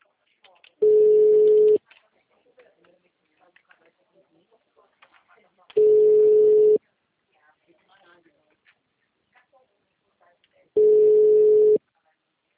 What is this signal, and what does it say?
Ringback tone (Europe, 425 Hz), with speech.
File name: out- -2010- - - .37574